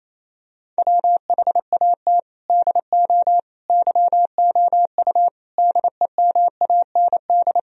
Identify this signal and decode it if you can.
Morse code, heard as a keyed tone, about 28 wpm, 700 Hz. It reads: WHAT DO YOU DEMAND
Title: Sentences from Top 1000 Words - Review Part 4 - 28wpm